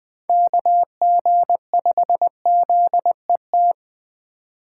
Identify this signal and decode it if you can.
Morse code, heard as a keyed tone, about 20 wpm, 700 Hz. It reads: KG5ZET